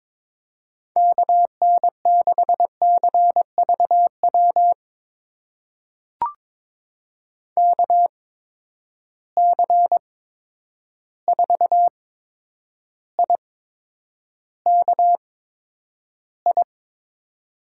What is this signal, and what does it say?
Morse code, heard as a keyed tone, about 22 wpm, spaced as for 9 wpm, 700 Hz. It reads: KN6CVW KC4IKI
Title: USA Call Signs - 6 Characters - 8x Character Spacing - 22wpm